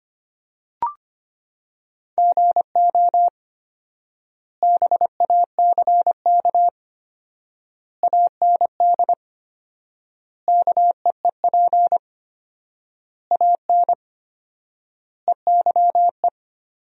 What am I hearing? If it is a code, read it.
Morse code, heard as a keyed tone, about 25 wpm, 700 Hz. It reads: GO BACK AND KEEP AN EYE
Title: 4x Word-Spacing - Sentences from Top 300 Words - 25wpm